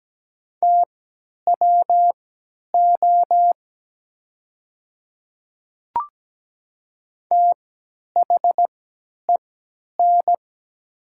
Morse code, heard as a keyed tone, about 17 wpm, spaced as for 10 wpm, 700 Hz. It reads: TWO THEN
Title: 3x Character Spacing - Top 200 Words - 17wpm